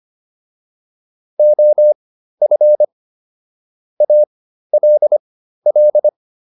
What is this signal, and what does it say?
Morse code, heard as a keyed tone, about 25 wpm, spaced as for 13 wpm, 600 Hz. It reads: OF ALL